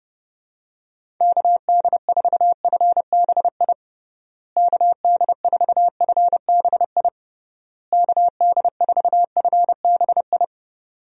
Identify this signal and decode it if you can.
Morse code, heard as a keyed tone, about 32 wpm, 700 Hz. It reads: KD4FBI KD4FBI KD4FBI
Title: USA Call Signs - 6 Characters - Repeated 3x - 30wpm